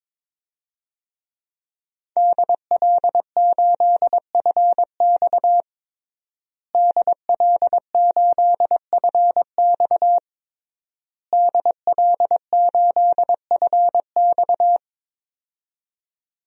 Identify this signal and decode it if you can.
Morse code, heard as a keyed tone, about 22 wpm, 700 Hz. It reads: DL8FX DL8FX DL8FX
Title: DXCC-100 Call Signs - 5 Characters - Repeated 3x - 22wpm